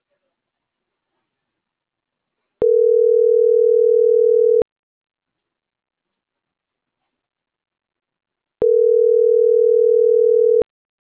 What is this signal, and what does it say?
Ringback tone (North America), 440 + 480 Hz.